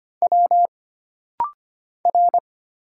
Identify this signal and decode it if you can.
Morse code, heard as a keyed tone, about 25 wpm, 700 Hz. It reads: W R
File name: Single Character (Letter:Number:Punctuation:Prosign) - Rapid-Fire - 25wpm